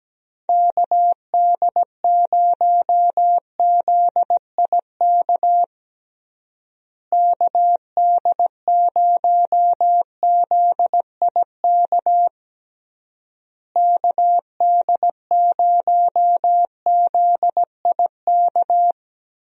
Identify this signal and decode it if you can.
Morse code, heard as a keyed tone, about 17 wpm, 700 Hz. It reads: KD0ZIK KD0ZIK KD0ZIK
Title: USA Call Signs - 6 Characters - Repeated 3x - 17wpm